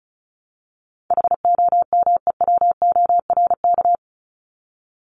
Morse code, heard as a keyed tone, about 34 wpm, 700 Hz. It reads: HOMEWORK
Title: Top 1501-1750 Words 35wpm